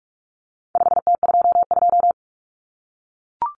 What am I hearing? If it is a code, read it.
Morse code, heard as a keyed tone, about 44 wpm, 700 Hz. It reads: 5T22
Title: Common Cut Numbers - 4-Digit 45wpm